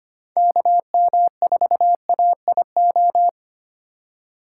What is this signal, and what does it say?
Morse code, heard as a keyed tone, about 25 wpm, 700 Hz. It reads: KM4AIO
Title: USA Call Signs - 6 Characters - 4x Character Spacing - 25wpm